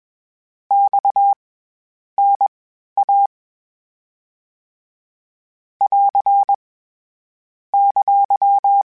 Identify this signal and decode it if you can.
Morse code, heard as a keyed tone, about 21 wpm, spaced as for 10 wpm, 800 Hz. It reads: XNA <AR> !
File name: CW Comparisons / CW Comparisons-mp3s / CW Comparisons Broken Characters @20